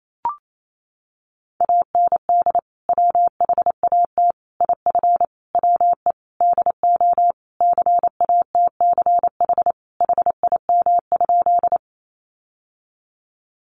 Morse code, heard as a keyed tone, about 28 wpm, 700 Hz. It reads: AND WHAT IF WE DO CATCH HIM?
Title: Sentences from Top 1500 Words 28wpm